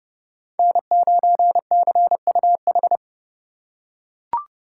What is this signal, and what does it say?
Morse code, heard as a keyed tone, about 29 wpm, 700 Hz. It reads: N9CUH